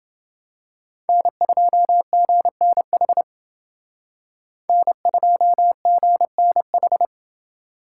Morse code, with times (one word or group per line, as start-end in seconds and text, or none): 1.1-3.2 N2GNH
4.7-7.0 N2GNH